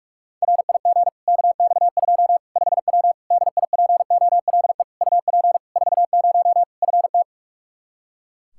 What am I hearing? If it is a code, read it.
Morse code, heard as a keyed tone, about 45 wpm, 700 Hz. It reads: RIG KX2 5W DIPOLE UP 40 FT
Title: Practice QSO - Exchange 2 - Info - 45wpm